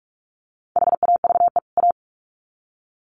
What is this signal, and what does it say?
Morse code, heard as a keyed tone, about 44 wpm, 700 Hz. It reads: HAVE A
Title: Sets of 2 Words 45wpm